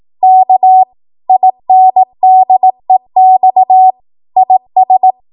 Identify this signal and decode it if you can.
Morse code, heard as a keyed tone, about 18 wpm, 750 Hz. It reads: K INDEX IS
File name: Morse / 18WPM